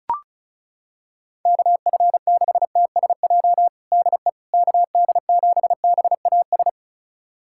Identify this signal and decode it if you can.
Morse code, heard as a keyed tone, about 34 wpm, 700 Hz. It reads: KF6TSJ DE KD7BAS
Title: USA Callsign DE Callsign - 6 Characters - 35wpm